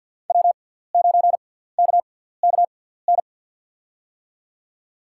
Morse code, heard as a keyed tone, about 48 wpm, spaced as for 17 wpm, 700 Hz. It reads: W9KKN
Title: USA Call Signs - 5 Characters - 6x Character Spacing - 50wpm